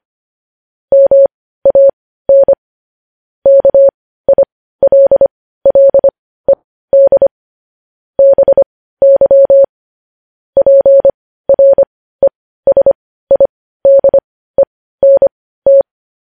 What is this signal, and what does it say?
Morse code, heard as a keyed tone, about 25 wpm, spaced as for 15 wpm, 550 Hz. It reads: MAN KILLED BY PRESIDENT
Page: news-headlines-in-morse-code-at-15-wpm